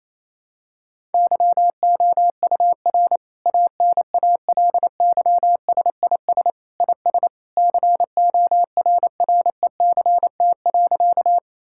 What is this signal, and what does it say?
Morse code, heard as a keyed tone, about 28 wpm, 700 Hz. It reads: YOUR ANALYSIS IS CORRECT.
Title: Sentences from Top 1000 Words - Review Part 5 - 28wpm